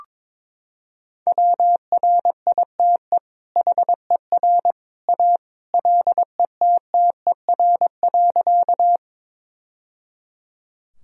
Morse code, heard as a keyed tone, about 21 wpm, 700 Hz. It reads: WRITE HER A LETTER.
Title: Sentences from Top 1000 Words - Review Part 1 - 22wpm